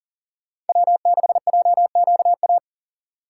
Morse code, heard as a keyed tone, about 40 wpm, 700 Hz. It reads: W6JQA
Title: USA Call Signs - 5 Characters - 2x Character Spacing - 40wpm